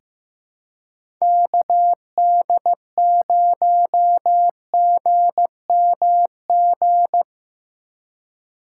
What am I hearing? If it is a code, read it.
Morse code, heard as a keyed tone, about 15 wpm, 700 Hz. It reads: KD0GMG
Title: USA Call Signs - 3x Character Spacing - 15wpm